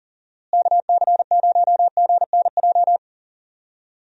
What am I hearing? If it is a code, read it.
Morse code, heard as a keyed tone, about 37 wpm, 700 Hz. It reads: KC0GNJ
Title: USA Call Signs - 6 Characters - Repeated 3x - 40wpm